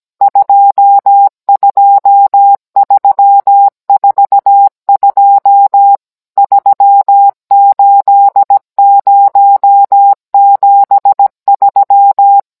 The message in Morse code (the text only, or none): 22342 38073